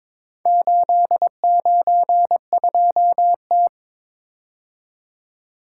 Morse code, heard as a keyed tone, about 23 wpm, 700 Hz. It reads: 892T
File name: practice / 22wpm / Four Letter-Number Combinations - 22wpm